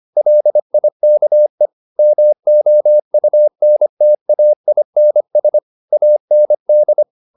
Morse code, heard as a keyed tone, about 25 wpm, 600 Hz. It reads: LIKE MOUNTAINS AND
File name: the old man and the sea part 1 25wpm